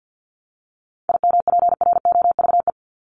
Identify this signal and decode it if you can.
Morse code, heard as a keyed tone, about 50 wpm, 700 Hz. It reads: IMPROVE
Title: Top 501-750 Words 50wpm